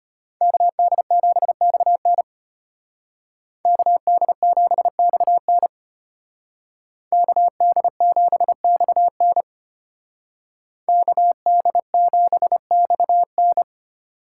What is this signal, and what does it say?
Morse code, heard as a keyed tone, about 33 wpm, 700 Hz. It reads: KD7XN KD7XN KD7XN KD7XN